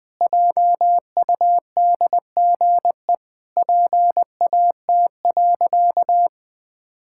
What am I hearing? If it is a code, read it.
Morse code, heard as a keyed tone, about 20 wpm, 700 Hz. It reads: JUDGE PAT.
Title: Pangram Challenge v2 - 20wpm